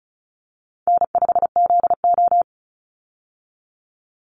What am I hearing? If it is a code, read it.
Morse code, heard as a keyed tone, about 34 wpm, 700 Hz. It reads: N5ZO
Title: Call Signs Repeated 3x - 35wpm